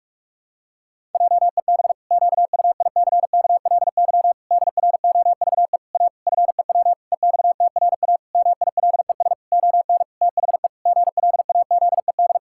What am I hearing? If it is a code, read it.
Morse code, heard as a keyed tone, about 48 wpm, 700 Hz. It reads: JEB QUICKLY DROVE A FEW EXTRA MILES ON THE GLAZED
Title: Pangram Challenge v2 - 45wpm